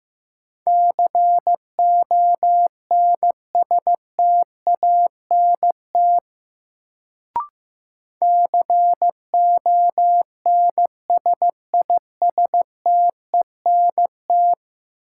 Morse code, heard as a keyed tone, about 15 wpm, 700 Hz. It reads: CONSTANT CONSISTENT
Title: Sentences from Top 1500 Words - New Word Drill Alt - 15wpm